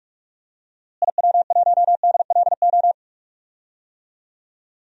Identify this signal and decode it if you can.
Morse code, heard as a keyed tone, about 43 wpm, 700 Hz. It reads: IW1DLO
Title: DXCC-100 Call Signs - 6 Characters - Repeated 2x - 45wpm